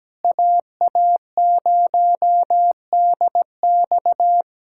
Morse code, heard as a keyed tone, about 17 wpm, 700 Hz. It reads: AA0DX
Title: USA Callsign DE Callsign 17wpm